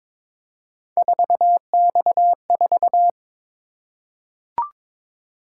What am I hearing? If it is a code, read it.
Morse code, heard as a keyed tone, about 22 wpm, 700 Hz. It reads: 4X4
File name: practice / 22wpm / Three Letter-Number Combinations - 22wpm